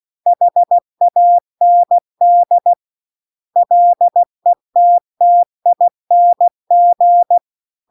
Morse code, heard as a keyed tone, about 16 wpm, 700 Hz. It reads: HAND LETTING